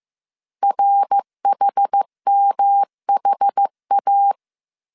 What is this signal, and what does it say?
Morse code, heard as a keyed tone, about 15 wpm, 800 Hz. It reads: RHMHA